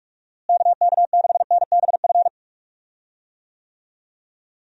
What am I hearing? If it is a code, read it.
Morse code, heard as a keyed tone, about 46 wpm, 700 Hz. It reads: KK6NBF
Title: USA Call Signs - 6 Characters - Repeated 2x - 45wpm